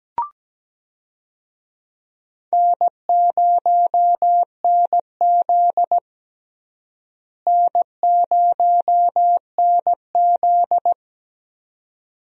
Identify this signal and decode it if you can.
Morse code, heard as a keyed tone, about 17 wpm, 700 Hz. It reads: N0NZ N0NZ
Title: USA Call Signs - 4 Characters - Repeated 2x - 17wpm